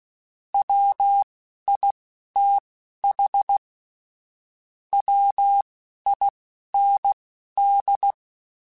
Morse code, heard as a keyed tone, about 16 wpm, spaced as for 11 wpm, 800 Hz. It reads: WITH WIND
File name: Weather Sets / Weather Sets-mp3s / Weather 5 5 @10